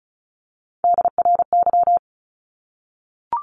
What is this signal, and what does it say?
Morse code, heard as a keyed tone, about 34 wpm, 700 Hz. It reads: DRY
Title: Top 500 Words 35wpm